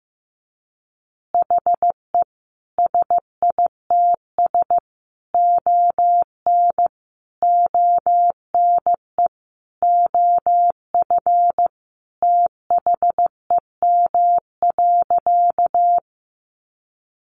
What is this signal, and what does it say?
Morse code, heard as a keyed tone, about 15 wpm, 700 Hz. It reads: HE SITS ON ONE OF THEM.
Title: Additional Easy Phrases 15wpm